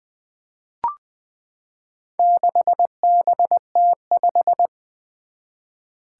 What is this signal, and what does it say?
Morse code, heard as a keyed tone, about 20 wpm, 700 Hz. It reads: 6BT5
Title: Four Challenging Letter-Number Combinations 20wpm